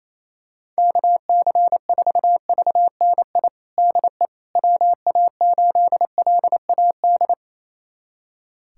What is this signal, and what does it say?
Morse code, heard as a keyed tone, about 28 wpm, 700 Hz. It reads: KC4VNI DE WA8LAD
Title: USA Callsign DE Callsign 28wpm